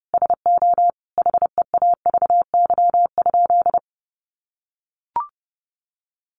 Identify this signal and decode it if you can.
Morse code, heard as a keyed tone, about 30 wpm, 700 Hz. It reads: SO HEAVY?
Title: Sentences from Top 1100 Words 30wpm